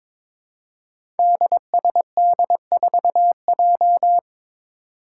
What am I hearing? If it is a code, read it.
Morse code, heard as a keyed tone, about 22 wpm, 700 Hz. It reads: DSD4J
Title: Five Challenging Letter-Number Combinations 22wpm